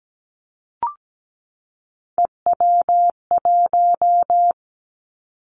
Morse code, heard as a keyed tone, about 17 wpm, 700 Hz. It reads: EW1
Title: Callsign Prefixes for Top 100 DXCC - 17wpm